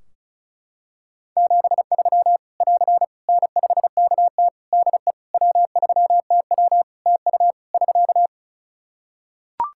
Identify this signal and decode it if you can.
Morse code, heard as a keyed tone, about 34 wpm, 700 Hz. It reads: 73 <AR> N5KT DE W3TW TU <SK>
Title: Practice QSO - Exchange 4 - Ending - 35wpm